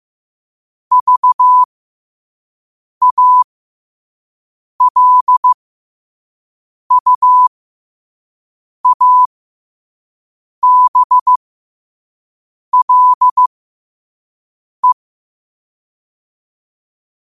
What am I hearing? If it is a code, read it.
Morse code, heard as a keyed tone, about 15 wpm, spaced as for 5 wpm, 1000 Hz. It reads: VALUABLE
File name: qotdfiles / 5wpm